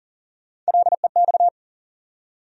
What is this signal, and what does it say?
Morse code, heard as a keyed tone, about 39 wpm, 700 Hz. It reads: LEX
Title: USA Call Sign Postfixes - 3 Letters 40wpm